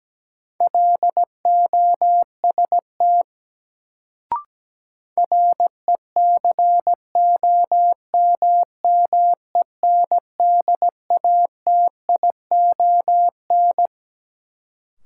Morse code, heard as a keyed tone, about 17 wpm, 700 Hz. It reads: LOST RECOMMENDATION
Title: Sentences from Top 1400 Words - New Word Drill - 17wpm